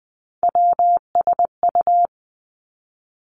Morse code, heard as a keyed tone, about 20 wpm, 700 Hz. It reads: WSU
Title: Three Challenging Letter-Number Combinations 20wpm